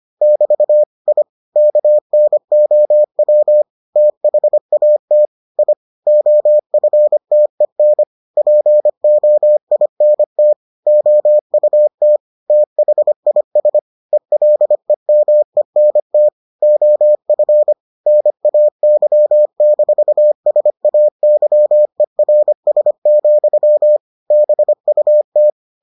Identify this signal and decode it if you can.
Morse code, heard as a keyed tone, about 25 wpm, 600 Hz. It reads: <BT> I KNOW THAT I OFTEN POINT OUT THIS ELEMENT OF NAY-SAYERS, BUT